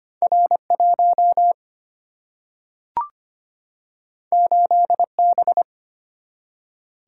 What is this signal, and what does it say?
Morse code, heard as a keyed tone, about 25 wpm, 700 Hz. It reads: R 1 8 B
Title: Single-Letter Single-Number 25wpm